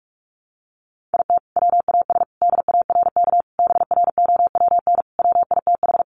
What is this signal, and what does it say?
Morse code, heard as a keyed tone, about 44 wpm, 700 Hz. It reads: IT WAS DARK BROWN WITH
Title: Sentences from Top 1400 Words 45wpm